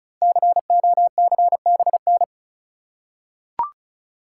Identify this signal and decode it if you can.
Morse code, heard as a keyed tone, about 36 wpm, 690 Hz. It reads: COCBN
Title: Code Groups - 5 Letters - 35wpm